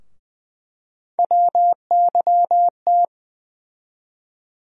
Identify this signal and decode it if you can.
Morse code, heard as a keyed tone, about 20 wpm, 700 Hz. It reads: WYT
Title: USA Call Sign Postfixes - 3 Letters 20wpm